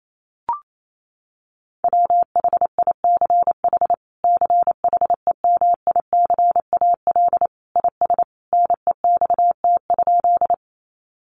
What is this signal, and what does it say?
Morse code, heard as a keyed tone, about 28 wpm, 700 Hz. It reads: WHICH CHEMICAL IS NEXT?
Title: Sentences from Top 1200 Words 28wpm